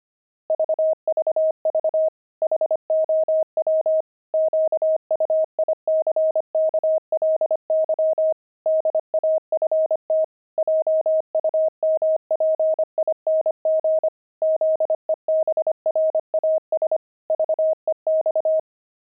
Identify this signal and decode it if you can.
Morse code, heard as a keyed tone, about 25 wpm, 630 Hz. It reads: VVV HOW QUICKLY DAFT JUMPING ZEBRAS VEX
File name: pangram-fs-25-cs-25-daft